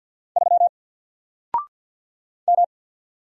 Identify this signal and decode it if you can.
Morse code, heard as a keyed tone, about 50 wpm, 700 Hz. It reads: 3 M
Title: Single Character (Letter:Number:Punctuation:Prosign) 50wpm